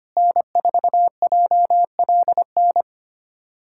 Morse code, heard as a keyed tone, about 25 wpm, 700 Hz. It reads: N4JLN
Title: USA Callsign DE Callsign - 5 Characters - 25wpm